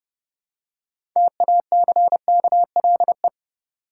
Morse code, heard as a keyed tone, about 32 wpm, 700 Hz. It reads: TACKLE
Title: Six Letter Words - 30wpm